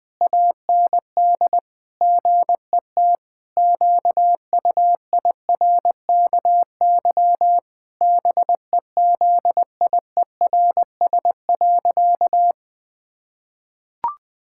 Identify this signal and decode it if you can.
Morse code, heard as a keyed tone, about 21 wpm, 700 Hz. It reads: AND GET QUIRKY BEZIERS.
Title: Pangram Challenge v2 - 20wpm